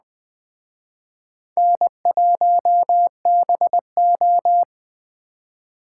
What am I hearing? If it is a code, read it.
Morse code, heard as a keyed tone, about 20 wpm, 700 Hz. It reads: N1BO